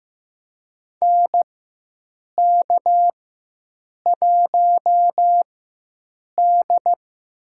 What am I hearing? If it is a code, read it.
Morse code, heard as a keyed tone, about 15 wpm, spaced as for 7 wpm, 700 Hz. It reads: NK1D
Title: USA Call Signs - 4 Characters - 4x Character Spacing - 15wpm